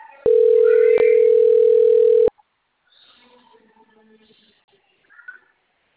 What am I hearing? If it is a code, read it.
Ringback tone (North America), 440 + 480 Hz.